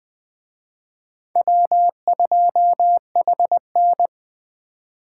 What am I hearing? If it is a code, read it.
Morse code, heard as a keyed tone, about 21 wpm, 700 Hz. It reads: W2HN
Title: Speed Racing - USA Call Signs 17wpm